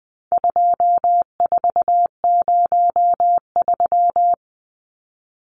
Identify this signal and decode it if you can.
Morse code, heard as a keyed tone, about 20 wpm, 700 Hz. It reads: 2403
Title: Numbers - 4-Digit 20wpm